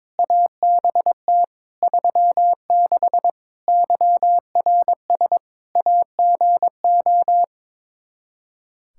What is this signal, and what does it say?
Morse code, heard as a keyed tone, about 22 wpm, 700 Hz. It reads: ABT 36 YRS AGO
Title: Practice QSO - Exchange 3 - More Info - 22wpm